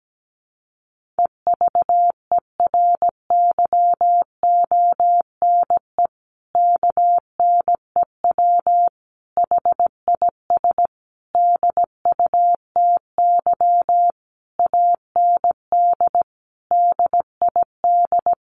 Morse code, heard as a keyed tone, about 17 wpm, 700 Hz. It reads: EVERYONE KNEW HIS DUTY AND DID
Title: Sentences from Top 1500 Words 17wpm